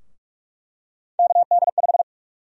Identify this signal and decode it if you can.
Morse code, heard as a keyed tone, about 44 wpm, 700 Hz. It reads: KD5
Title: USA Call Sign Prefixes - Repeated 2x - 45wpm